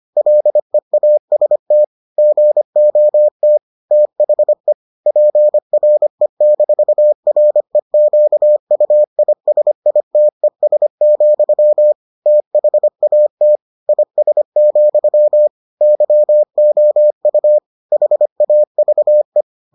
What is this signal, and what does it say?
Morse code, heard as a keyed tone, about 25 wpm, 600 Hz. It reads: LEAST GOT THE PRE-REQUISITES, THAT IS, YOU HAVE